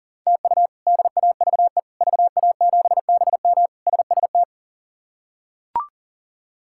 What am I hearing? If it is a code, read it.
Morse code, heard as a keyed tone, about 40 wpm, 700 Hz. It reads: TU DAVE VA7BM SST
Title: K1USN SST QSO Practice - 40wpm